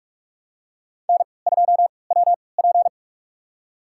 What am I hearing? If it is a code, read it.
Morse code, heard as a keyed tone, about 46 wpm, spaced as for 26 wpm, 700 Hz. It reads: N2WP
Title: USA Call Signs - 3x Character Spacing - 45wpm